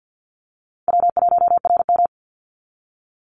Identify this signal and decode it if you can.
Morse code, heard as a keyed tone, about 50 wpm, 700 Hz. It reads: W1RM
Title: Call Signs - 50wpm